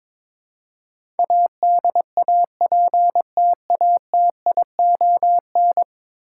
Morse code, heard as a keyed tone, about 22 wpm, 700 Hz. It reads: ADAPTATION
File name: practice / 22wpm / 10 Letter Words - First 3 Letters Given - 22wpm